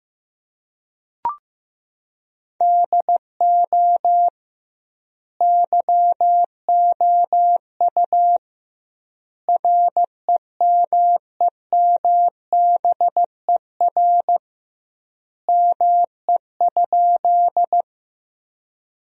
Morse code, heard as a keyed tone, about 15 wpm, 700 Hz. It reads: DO YOU REMEMBER ME?
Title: 2x Word-Spacing - Sentences from Top 400 Words - 15wpm